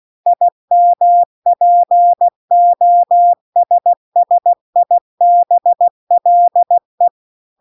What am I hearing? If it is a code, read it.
Morse code, heard as a keyed tone, about 16 wpm, 700 Hz. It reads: IMPOSSIBLE
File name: CW-The-Big-Twoheaded-River-2-16WPM